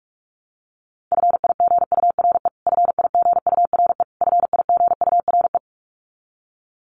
Morse code, heard as a keyed tone, about 44 wpm, 700 Hz. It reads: FIGURE FIGURE FIGURE
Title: Top 500 Words Repeated 45wpm